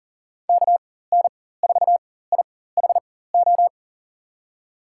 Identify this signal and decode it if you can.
Morse code, heard as a keyed tone, about 40 wpm, spaced as for 19 wpm, 700 Hz. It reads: KN4IHO
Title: USA Call Signs - 6 Characters - 4x Character Spacing - 40wpm